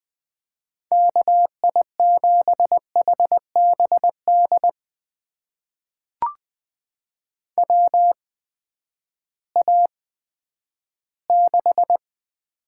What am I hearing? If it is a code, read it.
Morse code, heard as a keyed tone, about 20 wpm, spaced as for 11 wpm, 700 Hz. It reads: KI7HBD WA6
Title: USA Call Signs - 8x Character Spacing - 20wpm